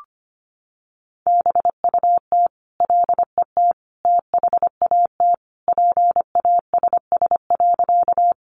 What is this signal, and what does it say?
Morse code, heard as a keyed tone, about 25 wpm, 700 Hz. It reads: BUT LET THAT PASS.
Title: Sentences from Top 400 Words 25wpm